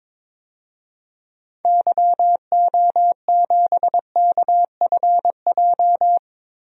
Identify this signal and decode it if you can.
Morse code, heard as a keyed tone, about 22 wpm, 700 Hz. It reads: YO7KFJ
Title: DXCC-100 Call Signs - 6 Characters - Repeated 2x - 22wpm